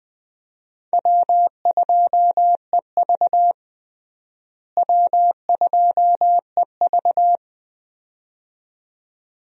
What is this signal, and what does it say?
Morse code, heard as a keyed tone, about 20 wpm, 700 Hz. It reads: W2EV W2EV
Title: USA Call Signs - 4 Characters - Repeated 2x - 20wpm